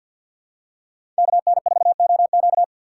Morse code, heard as a keyed tone, about 53 wpm, 700 Hz. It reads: KN4OQ
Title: USA Call Signs - 5 Characters - 6x Character Spacing - 50wpm